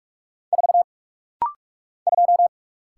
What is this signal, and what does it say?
Morse code, heard as a keyed tone, about 48 wpm, 690 Hz. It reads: 4 2
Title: Single Letter-Number - Rapid-Fire 45wpm